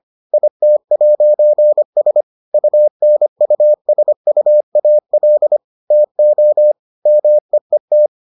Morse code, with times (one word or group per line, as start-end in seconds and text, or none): 0.3-2.2 IT'S
2.5-5.6 UNUSUAL
5.9-6.7 TO
7.1-8.1 MEET